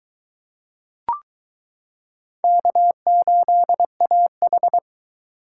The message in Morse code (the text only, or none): K8AH